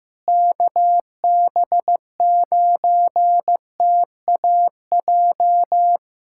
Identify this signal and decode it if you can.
Morse code, heard as a keyed tone, about 15 wpm, 700 Hz. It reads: KB9TAJ
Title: USA Callsign DE Callsign - 6 Characters - 15wpm